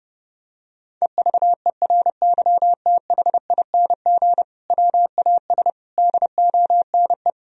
Morse code, heard as a keyed tone, about 32 wpm, 700 Hz. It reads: EVERYTHING WAS DONE
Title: Sentences from Top 1000 Words - Review Part 2 - 30wpm